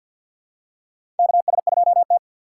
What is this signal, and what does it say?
Morse code, heard as a keyed tone, about 53 wpm, 700 Hz. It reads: KS2T
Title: USA Call Signs - 4 Characters - 2x Character Spacing - 50wpm